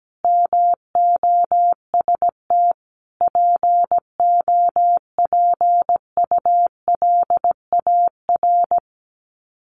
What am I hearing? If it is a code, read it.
Morse code, heard as a keyed tone, about 17 wpm, 700 Hz. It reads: MOST POPULAR
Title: Sets of 5 Words 17wpm